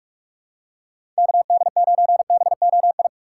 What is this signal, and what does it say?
Morse code, heard as a keyed tone, about 46 wpm, 700 Hz. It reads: KD9BOI
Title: USA Call Signs - 6 Characters - 3x Character Spacing - 45wpm